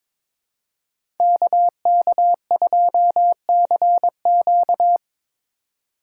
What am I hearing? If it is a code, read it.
Morse code, heard as a keyed tone, about 22 wpm, 700 Hz. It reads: KK2CQ
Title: USA Call Signs - 5 Characters - 4x Character Spacing - 22wpm